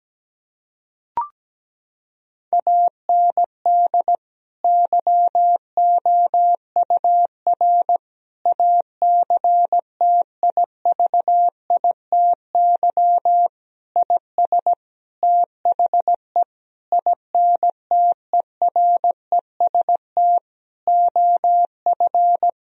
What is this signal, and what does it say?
Morse code, heard as a keyed tone, about 17 wpm, 700 Hz. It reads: AND YOUR ACTIVITY IS THE INTEREST OF
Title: Sentences from Top 1000 Words - Review Part 4 - 17wpm